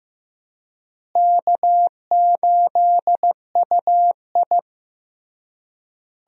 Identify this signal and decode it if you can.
Morse code, heard as a keyed tone, about 16 wpm, 700 Hz. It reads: K8UI
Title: USA Call Signs - 4 Characters - Repeated 3x - 15wpm